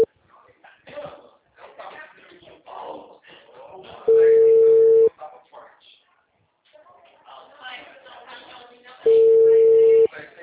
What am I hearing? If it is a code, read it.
Ringback tone (China, 450 Hz), with speech.